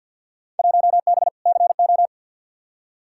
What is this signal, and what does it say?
Morse code, heard as a keyed tone, about 52 wpm, 700 Hz. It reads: 1B CO
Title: Field Day Exchanges - 50wpm